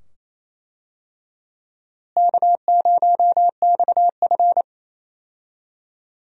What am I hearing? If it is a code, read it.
Morse code, heard as a keyed tone, about 28 wpm, 700 Hz. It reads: K0XF
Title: USA Call Signs - 4 Characters - Repeated 2x - 28wpm